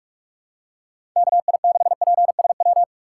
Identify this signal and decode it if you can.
Morse code, heard as a keyed tone, about 46 wpm, 700 Hz. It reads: KI6PSW
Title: USA Call Signs - 6 Characters - 2x Character Spacing - 45wpm